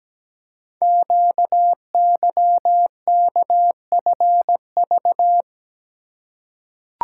Morse code, heard as a keyed tone, about 17 wpm, 700 Hz. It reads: QYKFV
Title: Code Groups - 5 Letters - 17wpm